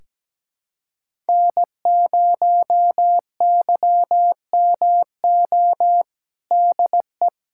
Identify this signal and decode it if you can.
Morse code, heard as a keyed tone, about 17 wpm, 700 Hz. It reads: N0YMO DE
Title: USA Callsign DE Callsign - 5 Characters - 17wpm